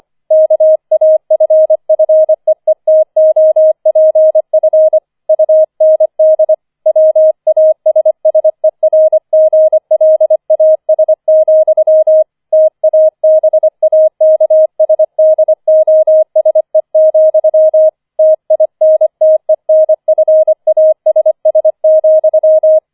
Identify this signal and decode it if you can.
Morse code, heard as a keyed tone, about 25 wpm, 630 Hz. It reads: KAFFEETOPF UND WASSERGLAS, TABAKSDOSE, TINTENFASS,
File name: 004 Vierter Streich 25wpm